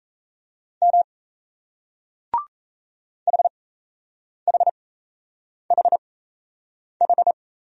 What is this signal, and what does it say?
Morse code, heard as a keyed tone, about 37 wpm, spaced as for 8 wpm, 700 Hz. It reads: M HHHH